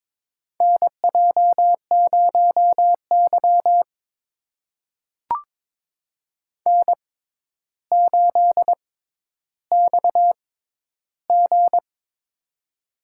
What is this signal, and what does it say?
Morse code, heard as a keyed tone, about 21 wpm, spaced as for 11 wpm, 700 Hz. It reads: NJ0Y N8XG
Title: USA Call Signs - 4 Characters - 6x Character Spacing - 22wpm